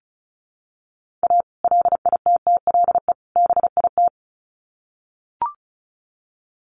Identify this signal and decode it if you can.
Morse code, heard as a keyed tone, about 34 wpm, 700 Hz. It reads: A LITTLE BIT
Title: Sets of 3 Words 35wpm